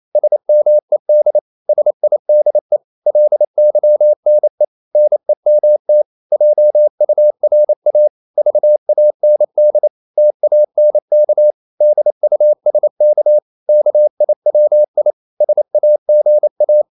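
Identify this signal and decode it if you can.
Morse code, heard as a keyed tone, about 28 wpm, 600 Hz. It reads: SMED SIDE LYNE NEMT JURA VAND TANK DUSK KIWI SAGA
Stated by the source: Danish short words sent in Morse at 28 wpm, from CW med Gnister.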